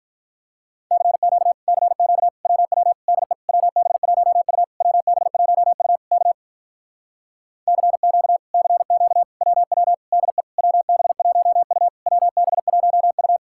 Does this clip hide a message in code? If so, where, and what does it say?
Yes: Morse code from 0.9 to 13.4 s, heard as a keyed tone, about 48 wpm, 700 Hz. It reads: CQ CQ WW DE WB1U WB1U K CQ CQ WW DE WB1U WB1U